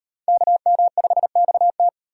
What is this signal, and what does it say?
Morse code, heard as a keyed tone, about 35 wpm, 700 Hz. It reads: KM5XT